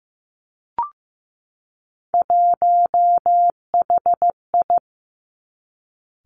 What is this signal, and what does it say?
Morse code, heard as a keyed tone, about 15 wpm, 700 Hz. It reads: 1HI
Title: Three Challenging Letter-Number Combinations 15wpm